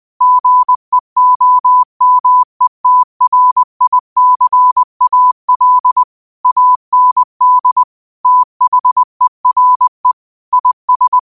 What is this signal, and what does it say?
Morse code, heard as a keyed tone, about 20 wpm, 1000 Hz. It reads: GEOMETRICAL AND THERE IS
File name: qotdfiles / 20wpm